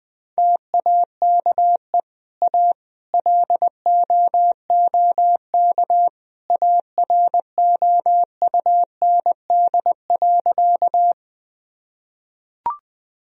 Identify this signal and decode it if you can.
Morse code, heard as a keyed tone, about 21 wpm, 700 Hz. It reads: TAKE A LOOK AROUND.
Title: Sentences from Top 1000 Words - Review Part 3 - 20wpm